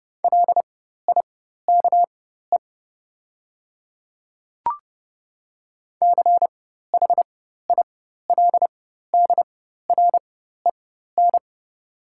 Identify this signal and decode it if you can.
Morse code, heard as a keyed tone, about 30 wpm, spaced as for 14 wpm, 700 Hz. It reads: LIKE CHILDREN
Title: 4x Character Spacing - Top 300 Words - 30wpm